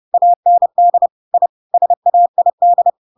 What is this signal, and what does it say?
Morse code, heard as a keyed tone, about 30 wpm, 700 Hz. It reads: AND I SAID
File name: Pinocchio / Pinocchio ch7 30 30 ik1hge